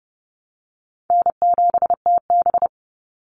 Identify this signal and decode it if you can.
Morse code, heard as a keyed tone, about 30 wpm, 700 Hz. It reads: N7TB
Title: Call Signs Repeated 2x - 30wpm